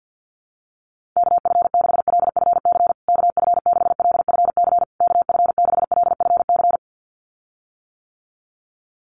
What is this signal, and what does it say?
Morse code, heard as a keyed tone, about 50 wpm, 700 Hz. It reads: KF6LFC KF6LFC KF6LFC
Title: USA Call Signs Repeated 3x 50wpm